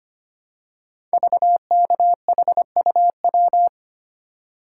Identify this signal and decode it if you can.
Morse code, heard as a keyed tone, about 25 wpm, 700 Hz. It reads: VKHUW
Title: Code Groups - 5 Letters - 25wpm